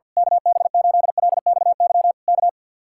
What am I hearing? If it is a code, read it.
Morse code, heard as a keyed tone, about 50 wpm, 700 Hz. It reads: KB8LXY K
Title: Speed-Racing Common QSO Phrases Encore - 50wpm